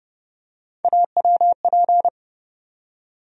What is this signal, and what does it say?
Morse code, heard as a keyed tone, about 32 wpm, 700 Hz. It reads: AWP
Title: USA Call Sign Postfixes - 3 Letters 30wpm